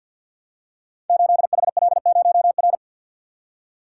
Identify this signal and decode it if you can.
Morse code, heard as a keyed tone, about 52 wpm, 700 Hz. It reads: 8HL0R
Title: Code Groups - 5 Letter:Numbers - 50wpm